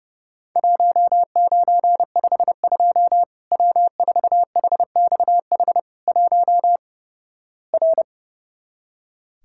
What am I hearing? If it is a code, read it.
Morse code, heard as a keyed tone, about 30 wpm, 700 Hz. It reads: 1952 W4HXH 1 R
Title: Worked All Europe DX Contest - QTC Exchange - 30wpm